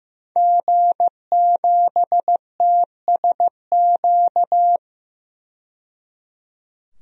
Morse code, heard as a keyed tone, about 15 wpm, 700 Hz. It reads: G7TSQ
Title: DXCC-100 Call Signs - 5 Characters - 15wpm